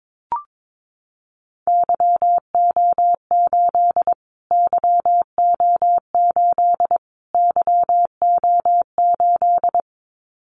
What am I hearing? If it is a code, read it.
Morse code, heard as a keyed tone, about 22 wpm, 700 Hz. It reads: YO8 YO8 YO8
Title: Callsign Prefixes for Top 25 DXCC - Repeated 3x - 22wpm